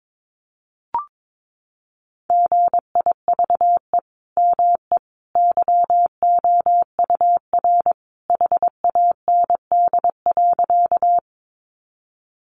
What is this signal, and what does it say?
Morse code, heard as a keyed tone, about 22 wpm, 700 Hz. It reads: GIVE ME YOUR HAND.
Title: Sentences from Top 200 Words 22wpm